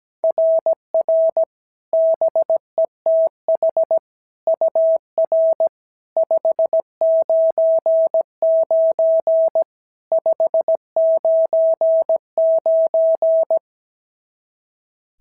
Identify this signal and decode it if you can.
Morse code, heard as a keyed tone, about 17 wpm, 690 Hz. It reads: RR BETH UR 599 599
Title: SOTA QSO Practice 17wpm